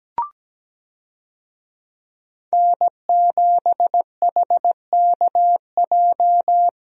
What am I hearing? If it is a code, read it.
Morse code, heard as a keyed tone, about 17 wpm, 700 Hz. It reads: N7HKJ
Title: USA Call Signs - 5 Characters - Repeated 3x - 17wpm